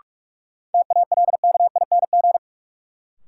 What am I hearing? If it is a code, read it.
Morse code, heard as a keyed tone, about 41 wpm, 700 Hz. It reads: TALKING
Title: Sets of 2 Words - New Word Drill - 45wpm